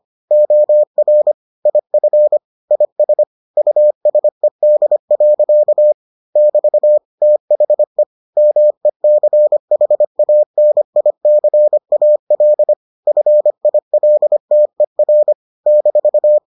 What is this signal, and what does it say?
Morse code, heard as a keyed tone, about 25 wpm, 600 Hz. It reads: OR IF IS USED. <BT> THE MECHANICAL FILTER -